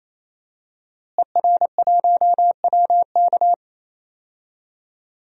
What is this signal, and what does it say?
Morse code, heard as a keyed tone, about 28 wpm, 700 Hz. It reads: ER1WK
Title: DXCC-100 Call Signs - 5 Characters - Repeated 3x - 28wpm